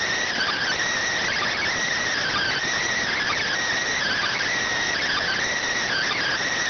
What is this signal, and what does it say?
Emergency Alert System (SAME) tones (header bursts), heard as harsh buzzing bursts. It is heard from 0.3 to 6.4 s.